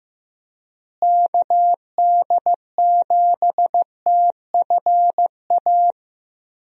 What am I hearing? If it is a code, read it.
Morse code, heard as a keyed tone, about 15 wpm, 700 Hz. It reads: KD7TFA
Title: USA Call Signs - 6 Characters - Repeated 3x - 15wpm